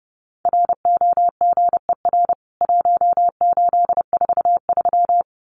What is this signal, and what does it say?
Morse code, heard as a keyed tone, about 30 wpm, 700 Hz. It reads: ROGER 1843
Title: CWT Exchange 30wpm